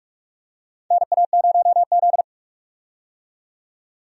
Morse code, heard as a keyed tone, about 42 wpm, 700 Hz. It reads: NA0Z
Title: USA Call Signs - 4 Characters - 45wpm